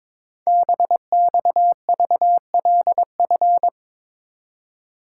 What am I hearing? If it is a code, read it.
Morse code, heard as a keyed tone, about 23 wpm, 700 Hz. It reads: BXVLF
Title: Code Groups - 5 Letters - 22wpm